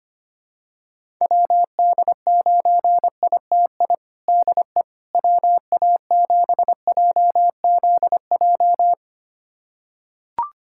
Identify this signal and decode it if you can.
Morse code, heard as a keyed tone, about 25 wpm, 700 Hz. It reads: WD9ITI DE WA7JZJ
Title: USA Callsign DE Callsign - 6 Characters - 25wpm